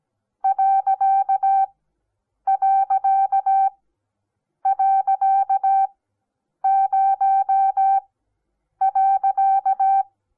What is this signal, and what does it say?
Morse code, heard as a keyed tone, about 17 wpm, spaced as for 8 wpm, 780 Hz. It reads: ...0.
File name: Onlinelehrgang / morsen / DJ4UF Morsen08